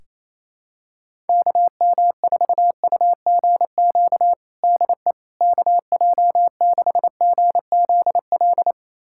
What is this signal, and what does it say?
Morse code, heard as a keyed tone, about 28 wpm, 700 Hz. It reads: KM4UGQ DE KJ6GZL
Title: USA Callsign DE Callsign 28wpm